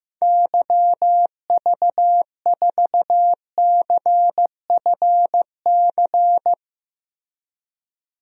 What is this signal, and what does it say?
Morse code, heard as a keyed tone, about 15 wpm, 700 Hz. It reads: YV4CFC
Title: DXCC-100 Call Signs - 6 Characters - Repeated 2x - 15wpm